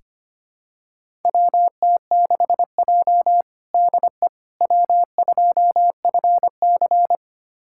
Morse code, heard as a keyed tone, about 25 wpm, 700 Hz. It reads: WT6J DE W2FC
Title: USA Callsign DE Callsign - 4 Characters - 25wpm